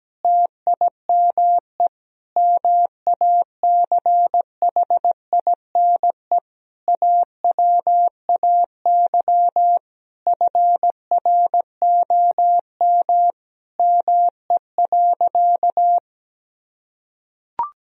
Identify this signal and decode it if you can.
Morse code, heard as a keyed tone, about 17 wpm, 700 Hz. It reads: TIME MACHINE AWAY FROM ME.
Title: Sentences from Top 1000 Words - Review Part 2 - 17wpm